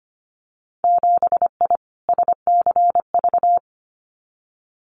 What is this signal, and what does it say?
Morse code, heard as a keyed tone, about 25 wpm, 700 Hz. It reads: 7I SCV
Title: Winter Field Day Exchanges Encore 25wpm